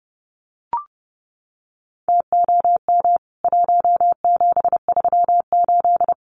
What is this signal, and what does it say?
Morse code, heard as a keyed tone, about 30 wpm, 700 Hz. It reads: TOM 1738
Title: CWT Exchange 30wpm